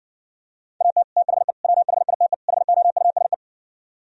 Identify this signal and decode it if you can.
Morse code, heard as a keyed tone, about 58 wpm, 700 Hz. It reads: AT THE WHITE HOUSE